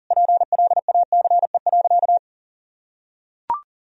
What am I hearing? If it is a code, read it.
Morse code, heard as a keyed tone, about 37 wpm, 700 Hz. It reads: PLACE.
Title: Sentences from Top 1000 Words - Review Part 1 - 40wpm